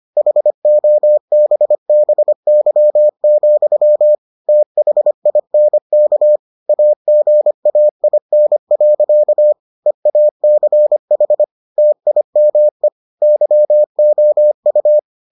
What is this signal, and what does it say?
Morse code, heard as a keyed tone, about 25 wpm, 600 Hz. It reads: HOBBY, THINK AGAIN. EACH TIME YOU